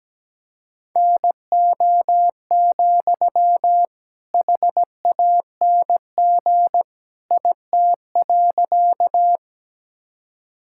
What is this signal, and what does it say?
Morse code, heard as a keyed tone, about 17 wpm, 700 Hz. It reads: NO, HANG IT.
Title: Sentences from Top 1600 Words 17wpm